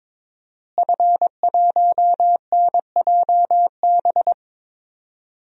Morse code, heard as a keyed tone, about 22 wpm, 700 Hz. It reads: F1NJB
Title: DXCC-100 Call Signs - 5 Characters - Repeated 2x - 22wpm